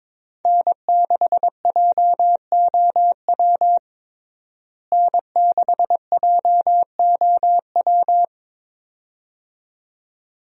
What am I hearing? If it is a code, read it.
Morse code, heard as a keyed tone, about 22 wpm, 700 Hz. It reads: N6JOW N6JOW
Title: USA Call Signs - 5 Characters - Repeated 2x - 22wpm